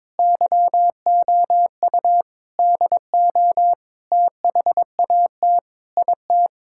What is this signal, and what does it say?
Morse code, heard as a keyed tone, about 22 wpm, 700 Hz. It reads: YOU DO THAT IT
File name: practice / 22wpm / Sentences with 3-Word Phrases - 22wpm